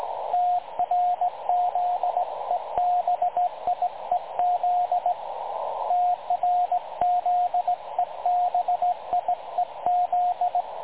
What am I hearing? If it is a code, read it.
Morse code, heard as a keyed tone, about 18 wpm, 720 Hz, with radio noise. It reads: TRZEBIEZ TRZEBIEZ
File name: dlazaawansowanych / nagraniazpasma / websdr recording 3565.0kHz